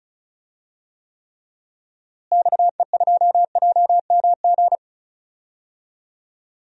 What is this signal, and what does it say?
Morse code, heard as a keyed tone, about 36 wpm, 700 Hz. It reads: XE2JMG
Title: DXCC-100 Call Signs - 6 Characters - Repeated 2x - 35wpm